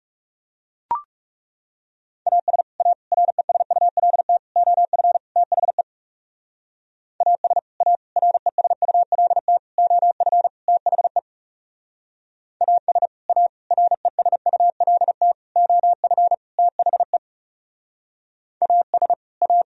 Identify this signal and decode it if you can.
Morse code, heard as a keyed tone, about 36 wpm, 700 Hz. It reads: AS A RESULT OF THE AS A RESULT OF THE AS A RESULT OF THE AS A
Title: Speed-Racing - Sets of 5 Words 30wpm